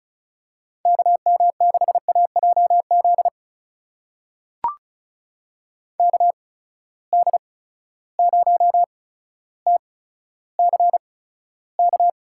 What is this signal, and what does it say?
Morse code, heard as a keyed tone, about 34 wpm, spaced as for 15 wpm, 700 Hz. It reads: KM6AJZ KD0TCK
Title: USA Call Signs - 6 Characters - 8x Character Spacing - 35wpm